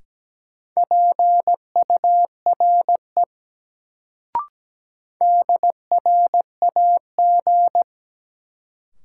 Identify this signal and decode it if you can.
Morse code, heard as a keyed tone, about 17 wpm, 700 Hz. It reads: PURE DRAG
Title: Sentences from Top 1500 Words - New Word Drill Alt - 17wpm